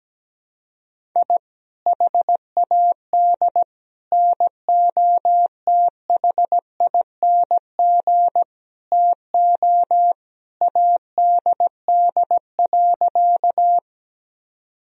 Morse code, heard as a keyed tone, about 17 wpm, 700 Hz. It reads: I HAD NOTHING TO ADD.
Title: Sentences from Top 1000 Words - Review Part 1 - 17wpm